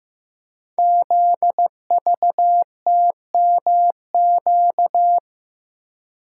Morse code, heard as a keyed tone, about 15 wpm, 700 Hz. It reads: ZVTMQ
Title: Code Groups - 5 Letters - 15wpm